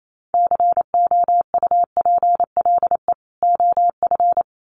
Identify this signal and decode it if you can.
Morse code, heard as a keyed tone, about 28 wpm, 700 Hz. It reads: COUPLE OF
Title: Sets of 2 Words 28wpm